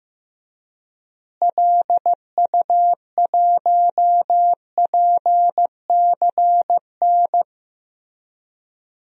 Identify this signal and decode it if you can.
Morse code, heard as a keyed tone, about 15 wpm, 700 Hz. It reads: LU1PCN
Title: DXCC-100 Call Signs - 6 Characters - Repeated 3x - 15wpm